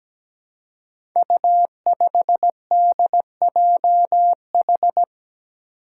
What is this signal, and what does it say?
Morse code, heard as a keyed tone, about 17 wpm, 700 Hz. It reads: U5DJH
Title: Five Challenging Letter-Number Combinations 17wpm